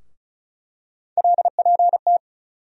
Morse code, heard as a keyed tone, about 34 wpm, 700 Hz. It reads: LPT
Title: USA Call Sign Postfixes - 3 Letters 35wpm